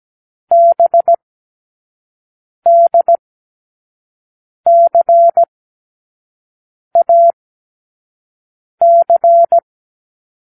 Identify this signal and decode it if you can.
Morse code, heard as a keyed tone, about 17 wpm, spaced as for 5 wpm, 680 Hz. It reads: BDCAC
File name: k7qo / Lesson6(A-D)